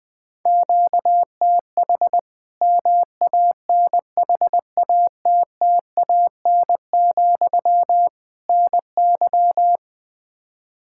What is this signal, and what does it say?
Morse code, heard as a keyed tone, about 20 wpm, 700 Hz. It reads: QTH MANHATTAN, NY
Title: Practice QSO - Exchange 1 - Beginning - 20wpm